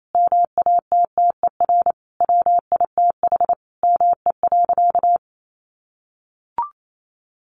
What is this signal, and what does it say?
Morse code, heard as a keyed tone, about 28 wpm, 700 Hz. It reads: MATTER WITH ME.
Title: Sentences from Top 700 Words 28wpm